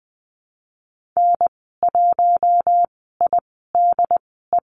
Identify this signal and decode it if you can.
Morse code, heard as a keyed tone, about 20 wpm, spaced as for 15 wpm, 700 Hz. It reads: N1IDE
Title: 076 Lesson 13g - Letter L - Call Signs 20wpm